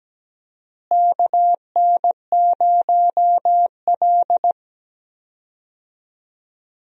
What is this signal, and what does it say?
Morse code, heard as a keyed tone, about 17 wpm, 700 Hz. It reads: KN0L
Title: USA Call Signs - 4 Characters - Repeated 3x - 17wpm